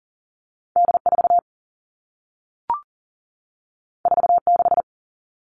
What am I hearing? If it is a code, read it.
Morse code, heard as a keyed tone, about 40 wpm, 700 Hz. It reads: D4 46
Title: All Cut Numbers - 2-Digit 40wpm